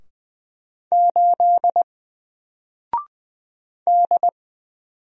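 Morse code, heard as a keyed tone, about 20 wpm, 700 Hz. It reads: 8 D
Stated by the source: Single letter and number Morse drill at 20 wpm, from N4LCD.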